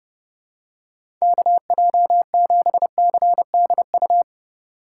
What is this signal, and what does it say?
Morse code, heard as a keyed tone, about 30 wpm, 700 Hz. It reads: KJ7CDU
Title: USA Call Signs - 6 Characters - 2x Character Spacing - 30wpm